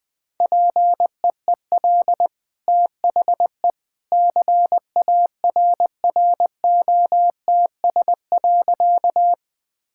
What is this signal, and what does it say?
Morse code, heard as a keyed tone, about 20 wpm, 700 Hz. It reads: PEEL THE CARROTS.
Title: Easy Sentences with Hints - 20wpm